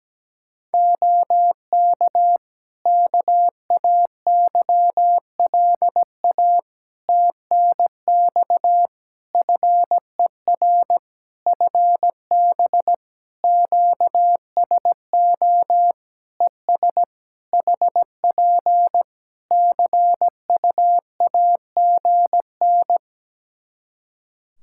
Morse code, heard as a keyed tone, about 17 wpm, 700 Hz. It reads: OK KAYLA TNX FER FB QSO ES HP CUAGN
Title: Practice QSO - Exchange 4 - Ending - 17wpm